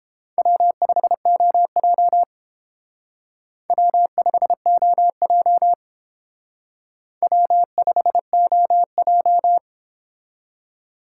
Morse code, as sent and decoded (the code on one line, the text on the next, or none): .-- ..... --- .--- / .-- ..... --- .--- / .-- ..... --- .---
W5OJ W5OJ W5OJ